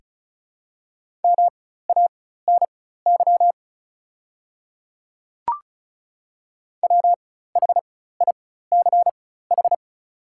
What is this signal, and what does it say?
Morse code, heard as a keyed tone, about 36 wpm, spaced as for 16 wpm, 690 Hz. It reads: MANY WHICH
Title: 4x Character Spacing - Top 100 Words - 35wpm